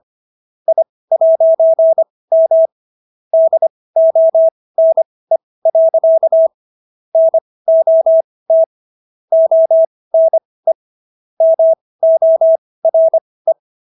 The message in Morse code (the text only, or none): I'M DONE. NOT ONE MORE